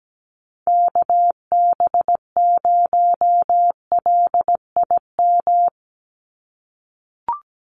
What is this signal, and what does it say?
Morse code, heard as a keyed tone, about 17 wpm, 700 Hz. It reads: KB0LIM
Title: USA Call Signs 17wpm